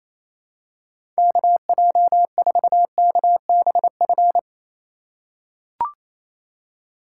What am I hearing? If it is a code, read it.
Morse code, heard as a keyed tone, about 28 wpm, 700 Hz. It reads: KJ4KBF
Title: USA Call Signs - 6 Characters - 4x Character Spacing - 28wpm